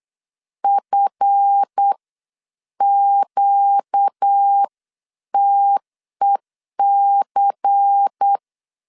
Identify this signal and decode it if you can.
Morse code, heard as a keyed tone, about 8 wpm, spaced as for 7 wpm, 800 Hz. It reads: FQTEC
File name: LEZIONI / 8 YJ45 / 8 YJ 8wpm 3313